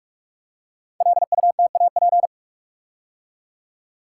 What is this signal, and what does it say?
Morse code, heard as a keyed tone, about 44 wpm, 700 Hz. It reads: LUTAP
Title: Code Groups - 5 Letters - 45wpm